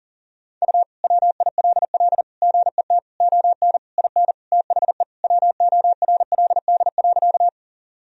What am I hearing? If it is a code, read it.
Morse code, heard as a keyed tone, about 40 wpm, 700 Hz. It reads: U WILL GET ON IN THE WORLD.
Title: Sentences from Top 200 Words and Abbreviations 40wpm